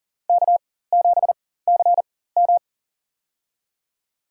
Morse code, heard as a keyed tone, about 40 wpm, spaced as for 19 wpm, 700 Hz. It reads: K7CM
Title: USA Call Signs - 4 Characters - 4x Character Spacing - 40wpm